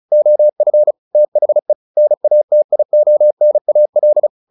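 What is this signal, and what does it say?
Morse code, heard as a keyed tone, about 34 wpm, 600 Hz. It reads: OF THE NATIONAL
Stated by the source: Morse practice at 35 wpm from CW med Gnister, sending Ken Follett's novel Never in English.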